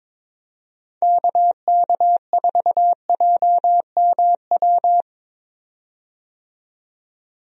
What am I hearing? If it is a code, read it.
Morse code, heard as a keyed tone, about 22 wpm, 700 Hz. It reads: KK4JMW
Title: USA Call Signs - 6 Characters - 22wpm